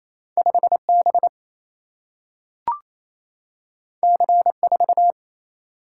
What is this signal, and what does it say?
Morse code, heard as a keyed tone, about 28 wpm, 700 Hz. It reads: 5 B C 4
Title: Single-Letter Single-Number 28wpm